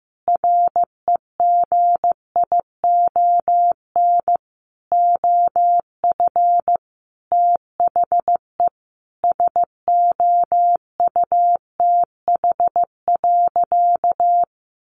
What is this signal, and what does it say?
Morse code, heard as a keyed tone, about 15 wpm, 700 Hz. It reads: REGION OF THE SOUTH.
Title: Sentences from Top 1500 Words 15wpm